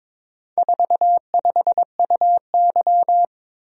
Morse code, heard as a keyed tone, about 22 wpm, 700 Hz. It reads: 45UY
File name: practice / 22wpm / Four Letter-Number Combinations - 22wpm